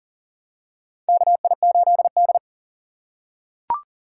Morse code, heard as a keyed tone, about 40 wpm, 700 Hz. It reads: KI8D
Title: USA Call Signs - 4 Characters - 2x Character Spacing - 40wpm